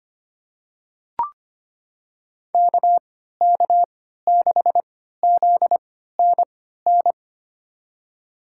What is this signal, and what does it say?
Morse code, heard as a keyed tone, about 25 wpm, spaced as for 14 wpm, 700 Hz. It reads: KK6ZNN
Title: USA Call Signs - 3x Character Spacing - 25wpm